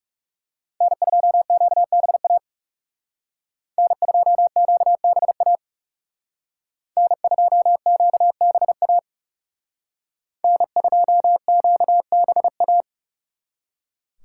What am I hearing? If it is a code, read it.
Morse code, heard as a keyed tone, about 39 wpm, 700 Hz. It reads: N2QBA N2QBA N2QBA N2QBA